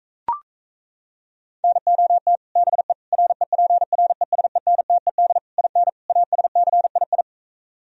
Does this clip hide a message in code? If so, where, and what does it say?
Yes: Morse code from 1.6 to 7.2 s, heard as a keyed tone, about 38 wpm, 700 Hz. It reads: NOT BE REPRESENTED IN ASCII